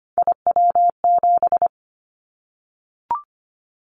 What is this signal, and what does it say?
Morse code, heard as a keyed tone, about 25 wpm, 700 Hz. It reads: IW7
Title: Callsign Prefixes for Top 25 DXCC - 25wpm